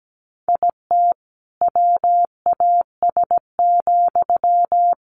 Morse code, heard as a keyed tone, about 17 wpm, 700 Hz. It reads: IT WAS,
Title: Sentences from Top 1400 Words 17wpm